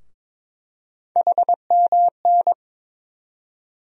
Morse code, heard as a keyed tone, about 22 wpm, 700 Hz. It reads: HMN
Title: USA Call Sign Postfixes - 3 Letters 22wpm